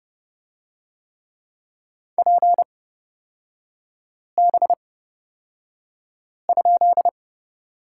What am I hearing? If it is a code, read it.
Morse code, heard as a keyed tone, about 29 wpm, spaced as for 5 wpm, 690 Hz. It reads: PB?
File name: course / 30:10wpm / 236 Lesson 38e - Letter q - Three Characters 30wpm